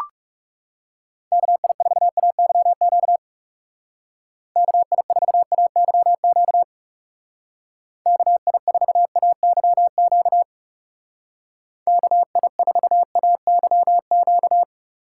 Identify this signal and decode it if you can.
Morse code, heard as a keyed tone, about 35 wpm, 700 Hz. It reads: KI4AYQ KI4AYQ KI4AYQ KI4AYQ
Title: Speed Racing - USA Call Signs 30wpm